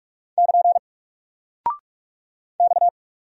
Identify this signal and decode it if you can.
Morse code, heard as a keyed tone, about 44 wpm, 700 Hz. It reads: <KN> X
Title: Single Character (Letter:Number:Punctuation:Prosign) Alt 45wpm